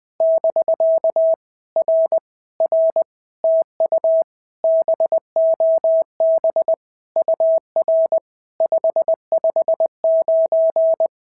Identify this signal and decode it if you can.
Morse code, heard as a keyed tone, about 19 wpm, 650 Hz. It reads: <BK> R R TU BOB UR 559
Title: POTA QSO Practice - 20wpm